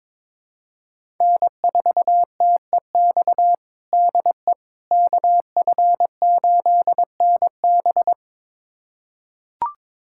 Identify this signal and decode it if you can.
Morse code, heard as a keyed tone, about 21 wpm, 700 Hz. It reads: N4TEX DE KF8NB
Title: USA Callsign DE Callsign - 5 Characters - 22wpm